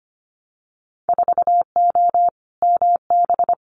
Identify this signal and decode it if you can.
Morse code, heard as a keyed tone, about 25 wpm, 700 Hz. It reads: 4O MB
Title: Winter Field Day Exchanges Encore 25wpm